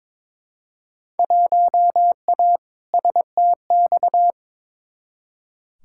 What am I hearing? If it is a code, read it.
Morse code, heard as a keyed tone, about 22 wpm, 700 Hz. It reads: 1A STX
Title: Field Day Exchanges Encore - 22wpm